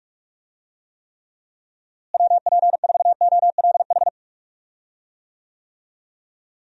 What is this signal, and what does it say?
Morse code, heard as a keyed tone, about 42 wpm, 700 Hz. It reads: WP4OLH
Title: USA Call Signs - 6 Characters - Repeated 3x - 45wpm